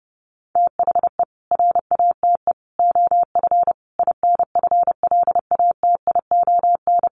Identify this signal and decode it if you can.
Morse code, heard as a keyed tone, about 30 wpm, 700 Hz. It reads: THE RATE OF INFLATION
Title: Sentences from Top 1100 Words 30wpm